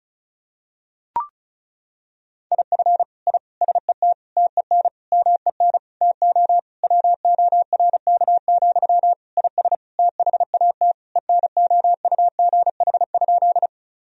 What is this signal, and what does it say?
Morse code, heard as a keyed tone, about 35 wpm, 700 Hz. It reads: IF I SET TEN MEN TO WORK, IS THAT ENOUGH?
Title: Sentences from Top 1000 Words - Review Part 2 - 35wpm